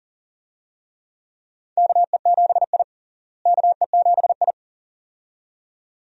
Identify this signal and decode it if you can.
Morse code, heard as a keyed tone, about 40 wpm, 700 Hz. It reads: KE7I KE7I
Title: USA Call Signs - 4 Characters - Repeated 2x - 40wpm